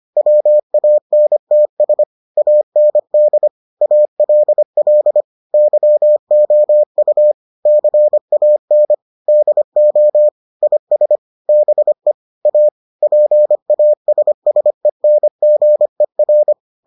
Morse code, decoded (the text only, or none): WANTS AND ALL YOU CAN DO IS BE A PASSENGER